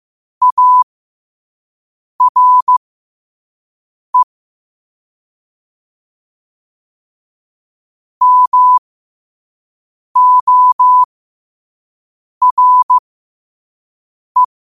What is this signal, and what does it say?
Morse code, heard as a keyed tone, about 15 wpm, spaced as for 5 wpm, 1000 Hz. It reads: ARE MORE
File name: qotdfiles / 5wpm